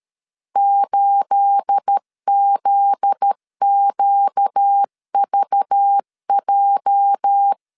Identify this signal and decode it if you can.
Morse code, heard as a keyed tone, about 13 wpm, 800 Hz. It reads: 8ZQVJ
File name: CW / DET / 1 9 1500C S A 12wpm 3800